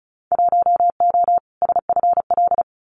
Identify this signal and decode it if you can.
Morse code, heard as a keyed tone, about 34 wpm, 690 Hz. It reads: 1O SFL
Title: Winter Field Day Exchanges 35wpm